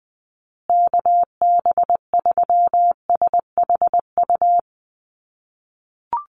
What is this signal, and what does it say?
Morse code, heard as a keyed tone, about 20 wpm, 700 Hz. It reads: KB3SHU
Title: USA Call Signs Repeated 3x 20wpm